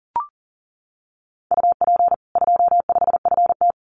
Morse code, heard as a keyed tone, about 40 wpm, 700 Hz. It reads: UP 25FT
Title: Common QSO Phrases 40wpm